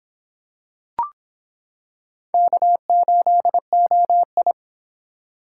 Morse code, heard as a keyed tone, about 26 wpm, 700 Hz. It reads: K8OI